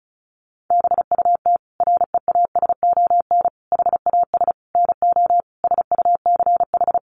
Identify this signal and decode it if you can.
Morse code, heard as a keyed tone, about 34 wpm, 700 Hz. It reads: BUT REASON HAS NO SUCH
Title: Sentences from Top 1100 Words 35wpm